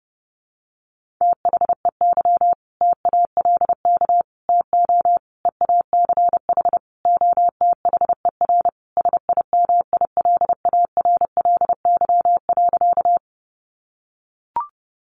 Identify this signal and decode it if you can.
Morse code, heard as a keyed tone, about 30 wpm, 700 Hz. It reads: THEY TALK TO EACH OTHER SIMILARLY.
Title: Sentences from Top 1400 Words 30wpm